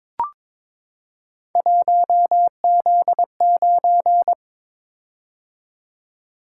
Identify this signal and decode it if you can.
Morse code, heard as a keyed tone, about 22 wpm, 700 Hz. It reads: 1Z9
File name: practice / 22wpm / Three Letter-Number Combinations - 22wpm